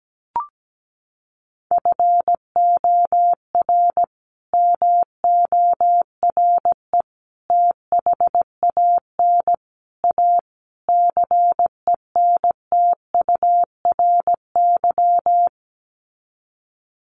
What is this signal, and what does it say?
Morse code, heard as a keyed tone, about 17 wpm, 700 Hz. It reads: FOR MORE THAN A CENTURY
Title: Sets of 5 Words Encore2 17wpm